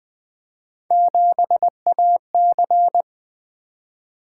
Morse code, heard as a keyed tone, about 21 wpm, 700 Hz. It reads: 7AC
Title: Three Letter-Number Combinations - 20wpm